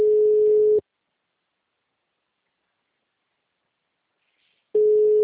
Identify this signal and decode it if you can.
Ringback tone (Europe, 425 Hz).